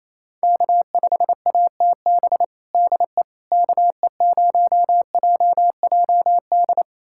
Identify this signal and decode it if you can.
Morse code, heard as a keyed tone, about 28 wpm, 700 Hz. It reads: K5ATB DE KE0JJD
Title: USA Callsign DE Callsign 28wpm